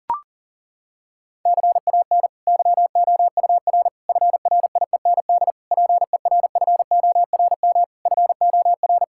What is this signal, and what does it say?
Morse code, heard as a keyed tone, about 40 wpm, 690 Hz. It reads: CAN YOUR FRIEND PERFORM FOR
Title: Sentences from Top 1000 Words - Review Part 5 - 40wpm